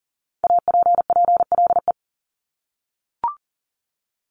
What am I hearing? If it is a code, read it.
Morse code, heard as a keyed tone, about 40 wpm, 700 Hz. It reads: APPLE
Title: Multi-letter Phonemes Repeated 3x 40wpm